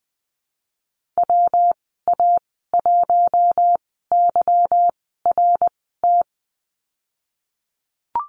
Morse code, heard as a keyed tone, about 20 wpm, spaced as for 15 wpm, 700 Hz. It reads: WA1YRT
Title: 154 Lesson 25g - Letter p - Call Signs 20wpm